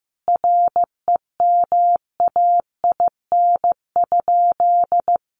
Morse code, heard as a keyed tone, about 15 wpm, 700 Hz. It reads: REMAIN?
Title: Sentences from Top 1200 Words 15wpm